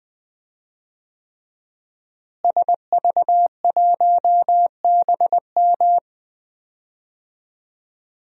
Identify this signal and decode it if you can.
Morse code, heard as a keyed tone, about 20 wpm, 700 Hz. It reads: SV1BM
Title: DXCC-100 Call Signs - 5 Characters - Repeated 3x - 20wpm